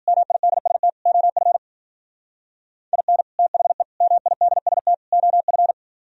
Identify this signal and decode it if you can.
Morse code, heard as a keyed tone, about 51 wpm, 700 Hz. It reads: MIDST OF IN THE MIDST OF